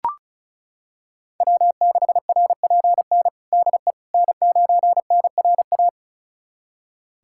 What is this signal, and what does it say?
Morse code, heard as a keyed tone, about 36 wpm, 700 Hz. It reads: W6RPN DE N9NRA
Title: USA Callsign DE Callsign - 5 Characters - 35wpm